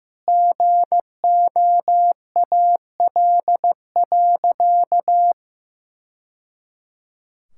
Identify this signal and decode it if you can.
Morse code, heard as a keyed tone, about 15 wpm, 700 Hz. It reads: GOAL.
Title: Sentences from Top 1000 Words - Review Part 6 - 15wpm